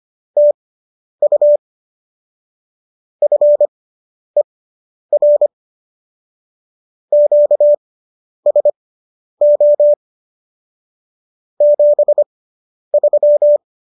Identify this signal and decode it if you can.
Morse code, heard as a keyed tone, about 25 wpm, spaced as for 10 wpm, 590 Hz. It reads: TU FER QSO 73
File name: qso005 10